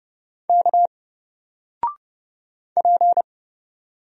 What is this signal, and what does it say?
Morse code, heard as a keyed tone, about 30 wpm, 700 Hz. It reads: K P
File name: Single-Letter-Number-30wpm